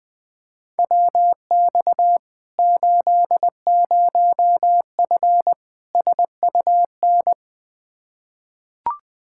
Morse code, heard as a keyed tone, about 20 wpm, 700 Hz. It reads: WX 80F SUN
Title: Practice QSO - Exchange 2 - Info - 20wpm